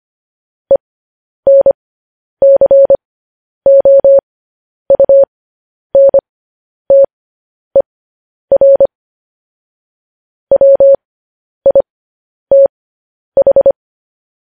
Morse code, heard as a keyed tone, about 25 wpm, spaced as for 10 wpm, 550 Hz. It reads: ENCOUNTER WITH
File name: podcast / News headlines-1x-25x10